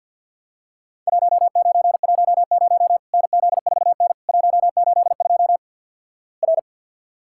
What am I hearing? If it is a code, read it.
Morse code, heard as a keyed tone, about 50 wpm, 700 Hz. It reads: 1910 NZ4N 182 R
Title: Worked All Europe DX Contest - QTC Exchange with Repeats - 50wpm